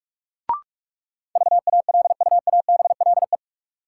Morse code, heard as a keyed tone, about 46 wpm, 700 Hz. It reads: VALUABLE
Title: Sentences from Top 1100 Words - New Word Drill Alt - 45wpm